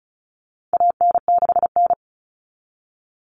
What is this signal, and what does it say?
Morse code, heard as a keyed tone, about 34 wpm, 700 Hz. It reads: AN6N
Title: Common Cut Numbers - 4-Digit 35wpm